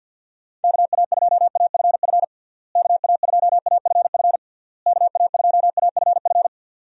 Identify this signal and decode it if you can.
Morse code, heard as a keyed tone, about 52 wpm, 700 Hz. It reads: KA2AFF KA2AFF KA2AFF
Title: USA Call Signs - 6 Characters - Repeated 3x - 50wpm